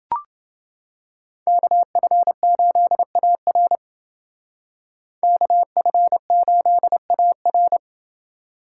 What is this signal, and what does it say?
Morse code, heard as a keyed tone, about 29 wpm, 700 Hz. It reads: KF8AR KF8AR